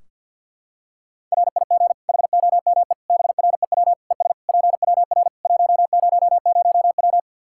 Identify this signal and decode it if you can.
Morse code, heard as a keyed tone, about 47 wpm, 700 Hz. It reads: RIG HOME BREW ES PWR 100W
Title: Practice QSO - Exchange 2 - Info - 50wpm